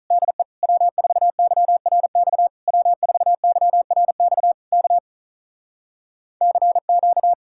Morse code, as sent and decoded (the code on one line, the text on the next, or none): -.. / .-- ....- -.-- .-. -..- / .-- ....- -.-- .-. -..- / -.- / -.-. --.-
DE W4YRX W4YRX K CQ